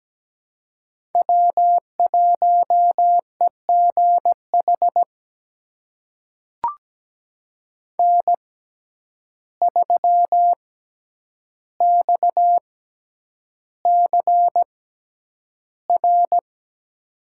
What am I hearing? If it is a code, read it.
Morse code, heard as a keyed tone, about 17 wpm, spaced as for 9 wpm, 700 Hz. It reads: W1EGH N3XCR
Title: USA Call Signs - 5 Characters - 6x Character Spacing - 17wpm